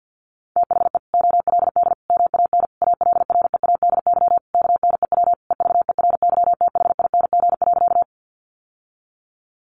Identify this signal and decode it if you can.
Morse code, heard as a keyed tone, about 48 wpm, 700 Hz. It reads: THE OLD MAN ALREADY KNEW EVERYTHING.
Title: Sentences from Top 700 Words 50wpm